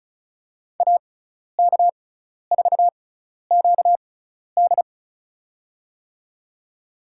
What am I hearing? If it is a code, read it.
Morse code, heard as a keyed tone, about 34 wpm, spaced as for 12 wpm, 700 Hz. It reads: AK4QD
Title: USA Call Signs - 5 Characters - 6x Character Spacing - 35wpm